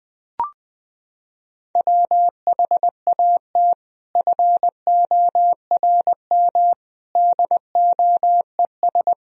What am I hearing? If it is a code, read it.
Morse code, heard as a keyed tone, about 20 wpm, 700 Hz. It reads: WHAT FORM DOES
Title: Sentences from Top 1000 Words - Review Part 1 - 20wpm